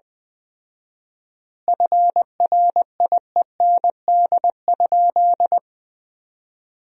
Morse code, heard as a keyed tone, about 20 wpm, 700 Hz. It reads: FRIEND?
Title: 4x Word-Spacing - Sentences from Top 300 Words - 20wpm